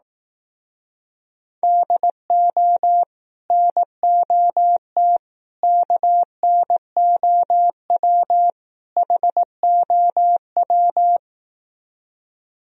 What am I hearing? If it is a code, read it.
Morse code, heard as a keyed tone, about 18 wpm, 700 Hz. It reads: DO NOT KNOW HOW
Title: Speed-Racing - Sets of 4 Words 15wpm